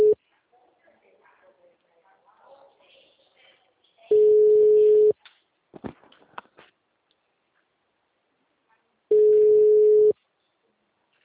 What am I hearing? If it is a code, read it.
Ringback tone (Europe, 425 Hz), with speech.